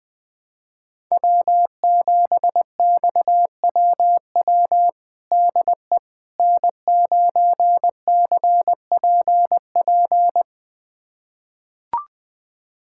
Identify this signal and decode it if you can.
Morse code, heard as a keyed tone, about 20 wpm, 700 Hz. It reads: W7XWW DE N9CPP
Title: USA Callsign DE Callsign - 5 Characters - 20wpm